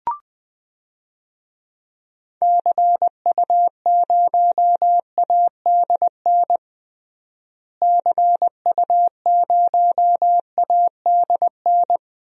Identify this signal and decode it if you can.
Morse code, heard as a keyed tone, about 21 wpm, 700 Hz. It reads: CU0ADN CU0ADN
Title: DXCC-100 Call Signs - 6 Characters - Repeated 3x - 20wpm